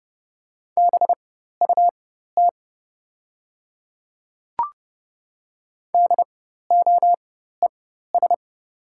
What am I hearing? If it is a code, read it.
Morse code, heard as a keyed tone, about 30 wpm, spaced as for 14 wpm, 700 Hz. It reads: BUT DOES
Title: 4x Character Spacing - Top 300 Words - 30wpm